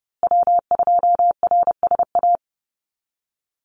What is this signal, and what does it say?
Morse code, heard as a keyed tone, about 30 wpm, 700 Hz. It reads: W2RSA
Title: USA Call Signs Repeated 3x 30wpm